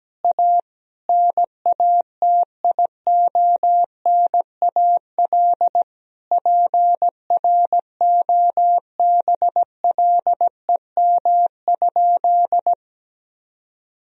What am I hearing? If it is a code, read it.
Morse code, heard as a keyed tone, about 17 wpm, 700 Hz. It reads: A NATIONAL PROBLEM?
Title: Sentences from Top 1000 Words - Review Part 4 - 17wpm